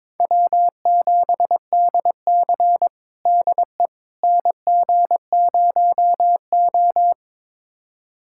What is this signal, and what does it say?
Morse code, heard as a keyed tone, about 22 wpm, 700 Hz. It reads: W7DC DE NG0O
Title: USA Callsign DE Callsign - 4 Characters - 22wpm